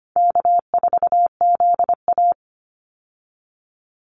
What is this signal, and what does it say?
Morse code, heard as a keyed tone, about 25 wpm, 700 Hz. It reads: K4ZA
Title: Easy Call Signs - 25wpm